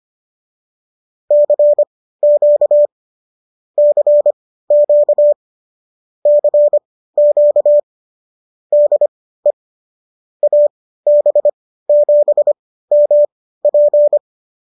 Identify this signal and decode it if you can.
Morse code, heard as a keyed tone, about 25 wpm, spaced as for 15 wpm, 590 Hz. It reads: CQ CQ CQ DE AB7MP